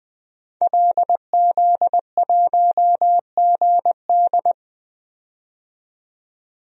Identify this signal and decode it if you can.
Morse code, heard as a keyed tone, about 20 wpm, 700 Hz. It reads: LZ1GD
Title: DXCC-100 Call Signs - 5 Characters - 20wpm